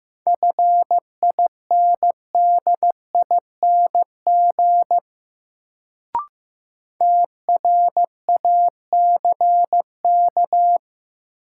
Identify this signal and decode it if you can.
Morse code, heard as a keyed tone, about 15 wpm, 700 Hz. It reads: FINDING TRACK
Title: Sentences from Top 1100 Words - New Word Drill - 15wpm